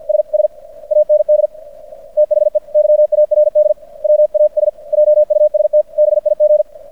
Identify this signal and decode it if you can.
Morse code, heard as a keyed tone, about 50 wpm, 600 Hz. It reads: SS AND THE YARD WAS JUST BIG
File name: archived / CRSnet